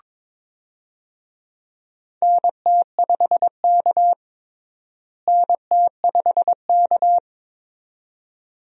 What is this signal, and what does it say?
Morse code, heard as a keyed tone, about 22 wpm, 700 Hz. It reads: NT5K NT5K
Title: USA Call Signs - 4 Characters - Repeated 2x - 22wpm